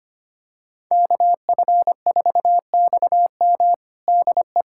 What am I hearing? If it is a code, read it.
Morse code, heard as a keyed tone, about 25 wpm, 700 Hz. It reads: KF4XM DE
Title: USA Callsign DE Callsign - 5 Characters - 25wpm